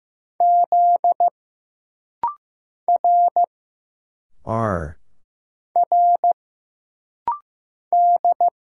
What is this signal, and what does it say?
Morse code, heard as a keyed tone, about 15 wpm, 700 Hz. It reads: Z R R D
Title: Single Letter-Number Alt - Rapid-Fire 15wpm